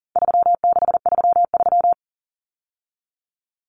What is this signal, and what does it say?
Morse code, heard as a keyed tone, about 40 wpm, 700 Hz. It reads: 3633
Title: Numbers - 4-Digit 40wpm